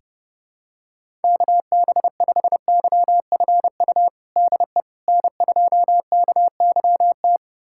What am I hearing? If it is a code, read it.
Morse code, heard as a keyed tone, about 30 wpm, 700 Hz. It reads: KB5YFU DE N2KYT
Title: USA Callsign DE Callsign 30wpm